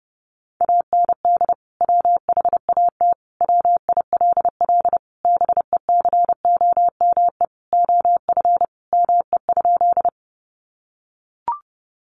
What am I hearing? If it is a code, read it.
Morse code, heard as a keyed tone, about 30 wpm, 700 Hz. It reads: AND WHAT WILL BECOME OF ME?
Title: Sentences from Top 600 Words 30wpm